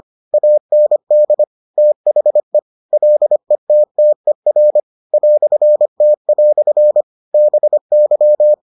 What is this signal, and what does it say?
Morse code, heard as a keyed tone, about 25 wpm, 600 Hz. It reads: AND THE LETTER "T" BY